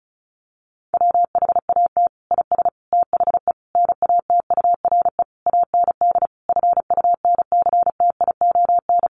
Morse code, heard as a keyed tone, about 34 wpm, 700 Hz. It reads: WHAT IS THE NATURE AND FUNCTION
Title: Sentences from Top 1100 Words 35wpm